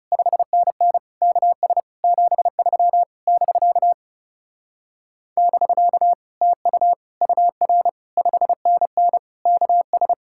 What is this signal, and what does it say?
Morse code, heard as a keyed tone, about 39 wpm, 700 Hz. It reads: 5NN KS 73 <BK> <BK> TU UR 5NN KS